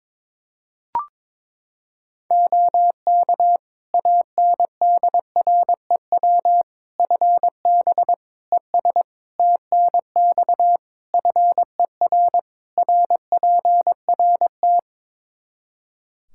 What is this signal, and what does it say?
Morse code, heard as a keyed tone, about 22 wpm, 700 Hz. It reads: OK ANDREW FB ES TNX FER RPRT
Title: Practice QSO - Exchange 2 - Info - 22wpm